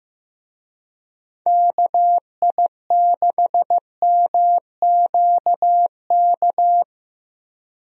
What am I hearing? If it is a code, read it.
Morse code, heard as a keyed tone, about 15 wpm, 700 Hz. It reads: KI6MQK
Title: USA Call Signs - 6 Characters - Repeated 3x - 15wpm